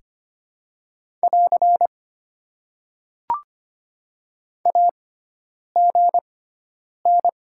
Morse code, heard as a keyed tone, about 25 wpm, spaced as for 9 wpm, 700 Hz. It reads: <AR> AGN
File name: Top 90 QSO Elements - 6x Character Spacing - 25wpm